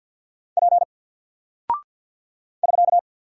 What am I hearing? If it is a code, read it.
Morse code, heard as a keyed tone, about 48 wpm, 700 Hz. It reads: P <SK>
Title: Single Character (Letter:Number:Punctuation:Prosign) Alt 50wpm